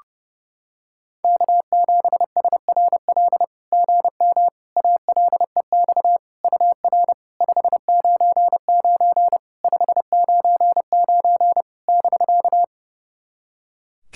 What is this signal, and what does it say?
Morse code, heard as a keyed tone, about 30 wpm, 660 Hz. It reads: K7SRL GM ALEX UR 599 599 <BK>
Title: SOTA QSO Practice 30wpm